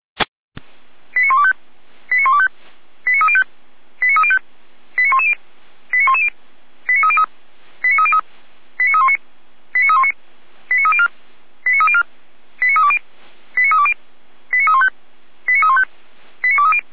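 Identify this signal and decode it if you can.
Five-tone selective call (ZVEI), heard as quick runs of five tones, tones of about 70 ms. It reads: 89125 89125 89305 89305 89110 89110 89303 89303 89319 89319 89304 89304 89320 89320 89315 89315 89230
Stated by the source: ZVEI five-tone selective call received on SDR, from Feuerwehr Orlamünde.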